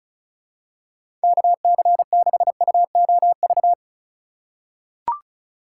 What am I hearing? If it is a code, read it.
Morse code, heard as a keyed tone, about 36 wpm, 700 Hz. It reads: KC6UOV
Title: USA Call Signs - 6 Characters - 4x Character Spacing - 35wpm